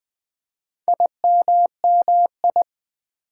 Morse code, heard as a keyed tone, about 20 wpm, 700 Hz. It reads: IMMI
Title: Build A Word - 4 Letters At A Time - 11-Letter Words - 20wpm